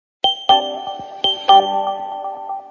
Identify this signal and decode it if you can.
There is no code.